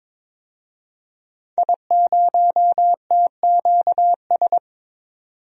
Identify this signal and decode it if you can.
Morse code, heard as a keyed tone, about 23 wpm, 700 Hz. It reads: I0TQS
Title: DXCC-100 Call Signs - 5 Characters - 22wpm